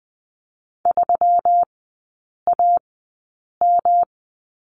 Morse code, heard as a keyed tone, about 20 wpm, spaced as for 8 wpm, 700 Hz. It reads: 3AM
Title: 138 Lesson 23e - Letter f - Three Characters 20wpm